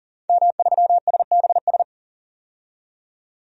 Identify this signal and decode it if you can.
Morse code, heard as a keyed tone, about 40 wpm, 700 Hz. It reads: M3SBS
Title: DXCC-100 Call Signs - 5 Characters - 40wpm